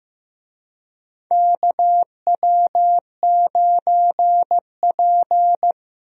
Morse code, heard as a keyed tone, about 15 wpm, 700 Hz. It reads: KW9P
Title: USA Call Signs - 4 Characters - Repeated 3x - 15wpm